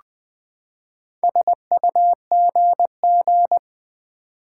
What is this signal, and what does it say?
Morse code, heard as a keyed tone, about 20 wpm, 700 Hz. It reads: SUGG
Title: Build A Word - 4 Letters At A Time - 10-Letter Words - 20wpm